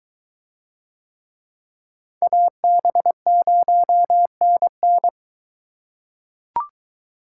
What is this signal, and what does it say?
Morse code, heard as a keyed tone, about 22 wpm, 700 Hz. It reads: AB0NN